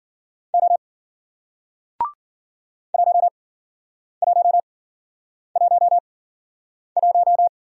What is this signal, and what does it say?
Morse code, heard as a keyed tone, about 53 wpm, spaced as for 9 wpm, 700 Hz. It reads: O 1111